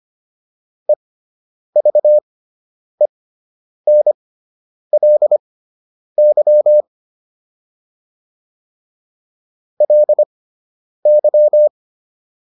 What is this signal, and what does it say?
Morse code, heard as a keyed tone, about 25 wpm, spaced as for 9 wpm, 600 Hz. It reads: EVENLY LY